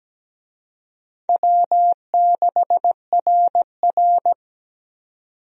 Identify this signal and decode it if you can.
Morse code, heard as a keyed tone, about 17 wpm, 700 Hz. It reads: W6RR
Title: USA Call Signs - 4 Characters - 17wpm